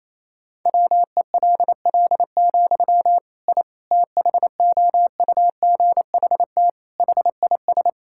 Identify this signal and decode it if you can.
Morse code, heard as a keyed tone, about 28 wpm, 700 Hz. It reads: WELL, I THOUGHT HIS
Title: Sentences from Top 1000 Words - Review Part 5 - 28wpm